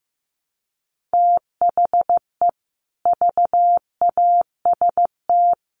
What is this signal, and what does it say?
Morse code, heard as a keyed tone, about 15 wpm, 700 Hz. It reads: THE VAST
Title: Sets of 4 Words 15wpm